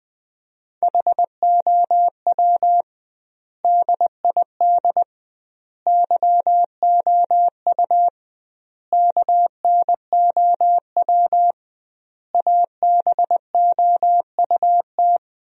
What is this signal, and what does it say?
Morse code, heard as a keyed tone, about 20 wpm, 700 Hz. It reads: HOW DID YOU KNOW ABOUT
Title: 2x Word-Spacing - Sentences from Top 100 Words - 20wpm